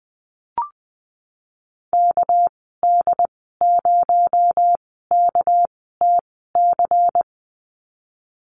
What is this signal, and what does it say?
Morse code, heard as a keyed tone, about 20 wpm, spaced as for 15 wpm, 700 Hz. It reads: KD0KTC
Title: 238 Lesson 38g - Letter q - Call Signs 20wpm